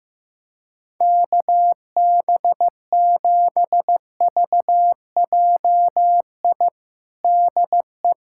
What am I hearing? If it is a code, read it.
Morse code, heard as a keyed tone, about 16 wpm, 700 Hz. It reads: KB7VJI DE
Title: USA Callsign DE Callsign - 6 Characters - 15wpm